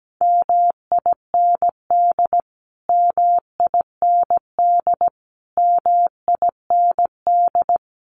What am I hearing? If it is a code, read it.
Morse code, heard as a keyed tone, about 17 wpm, 700 Hz. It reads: MIND MIND MIND
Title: Top 500 Words Repeated 17wpm